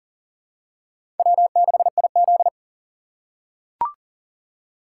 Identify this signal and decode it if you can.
Morse code, heard as a keyed tone, about 40 wpm, 700 Hz. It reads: W6IZ
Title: USA Call Signs - 4 Characters - 3x Character Spacing - 40wpm